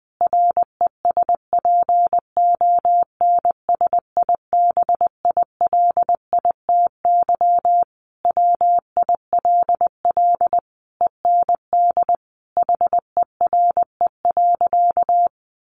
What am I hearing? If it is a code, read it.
Morse code, heard as a keyed tone, about 19 wpm, 700 Hz. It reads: RESPONSIBILITY WILL END HERE.
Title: Sentences from Top 1200 Words 20wpm